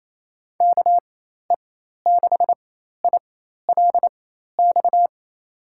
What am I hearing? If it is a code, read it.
Morse code, heard as a keyed tone, about 28 wpm, spaced as for 13 wpm, 700 Hz. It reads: KE6ILX
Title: USA Call Signs - 6 Characters - 4x Character Spacing - 28wpm